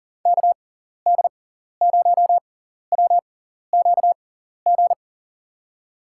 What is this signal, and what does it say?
Morse code, heard as a keyed tone, about 40 wpm, spaced as for 14 wpm, 700 Hz. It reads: KD0WQG
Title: USA Call Signs - 6 Characters - 6x Character Spacing - 40wpm